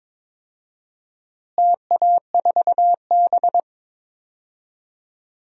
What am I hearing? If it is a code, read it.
Morse code, heard as a keyed tone, about 22 wpm, 700 Hz. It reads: TA4B
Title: DXCC-100 Call Signs - 4 Characters - Repeated 2x - 22wpm